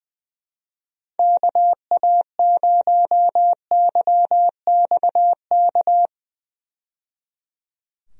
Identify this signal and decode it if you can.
Morse code, heard as a keyed tone, about 21 wpm, 700 Hz. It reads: KA0YXK
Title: USA Call Signs - 6 Characters - 20wpm